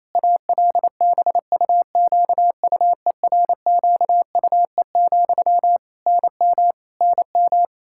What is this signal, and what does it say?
Morse code, heard as a keyed tone, about 28 wpm, 700 Hz. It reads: ALBUQUERQUE, NM NM
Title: Practice QSO - Exchange 1 - Beginning - 28wpm